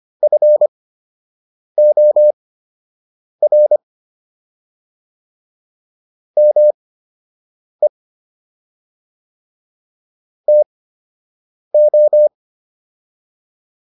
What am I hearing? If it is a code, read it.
Morse code, heard as a keyed tone, about 25 wpm, spaced as for 7 wpm, 600 Hz. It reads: FOR ME TO